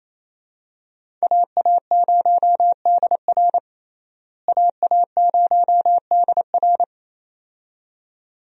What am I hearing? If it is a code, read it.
Morse code, heard as a keyed tone, about 28 wpm, 700 Hz. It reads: AA0DR AA0DR
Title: USA Call Signs - 5 Characters - Repeated 2x - 28wpm